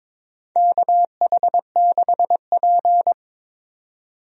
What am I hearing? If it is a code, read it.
Morse code, heard as a keyed tone, about 23 wpm, 700 Hz. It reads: KH6P
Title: DXCC-100 Call Signs - 4 Characters - 22wpm